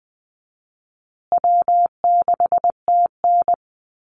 Morse code, heard as a keyed tone, about 20 wpm, 700 Hz. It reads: W6TN
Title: Easy Call Signs Repeated 3x - 20wpm